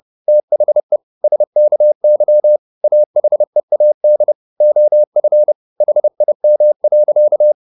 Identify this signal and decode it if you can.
Morse code, heard as a keyed tone, about 30 wpm, 600 Hz. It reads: THE SKY AHEAD OF HIM.
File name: the old man and the sea / the old man and the sea 1 30wpm